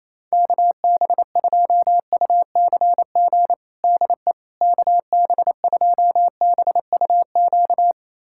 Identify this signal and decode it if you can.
Morse code, heard as a keyed tone, about 28 wpm, 700 Hz. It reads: KB2UCG DE KB2BUQ
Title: USA Callsign DE Callsign - 6 Characters - 28wpm